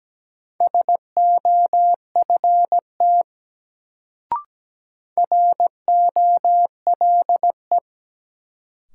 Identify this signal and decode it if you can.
Morse code, heard as a keyed tone, about 17 wpm, 700 Hz. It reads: SOFT ROLE
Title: Sentences from Top 1500 Words - New Word Drill Alt - 17wpm